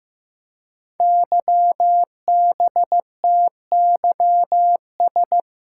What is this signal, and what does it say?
Morse code, heard as a keyed tone, about 15 wpm, 700 Hz. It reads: YBTYS
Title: Code Groups - 5 Letters - 15wpm